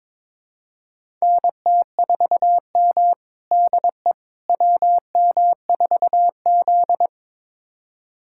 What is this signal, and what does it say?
Morse code, heard as a keyed tone, about 21 wpm, 700 Hz. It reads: NT4M DE WM4Z
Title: USA Callsign DE Callsign - 4 Characters - 22wpm